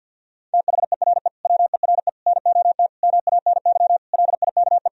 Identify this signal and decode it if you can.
Morse code, heard as a keyed tone, about 52 wpm, 700 Hz. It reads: THERE WERE NOT MANY LIKE
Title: Sentences from Top 1000 Words - Review Part 1 - 50wpm